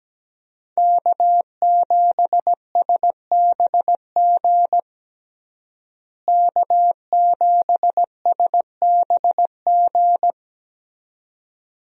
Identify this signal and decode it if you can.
Morse code, heard as a keyed tone, about 17 wpm, 700 Hz. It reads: K7SBG K7SBG
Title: USA Call Signs - 5 Characters - Repeated 2x - 17wpm